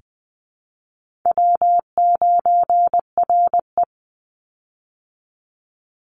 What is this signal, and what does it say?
Morse code, heard as a keyed tone, about 20 wpm, 700 Hz. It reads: W9RE
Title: Easy Call Signs Repeated 2x - 20wpm